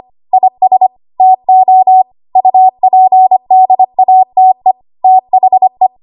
Morse code, heard as a keyed tone, about 25 wpm, 750 Hz. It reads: IS TO UPDATE THE
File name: Archive / 25 WPM / 25WPM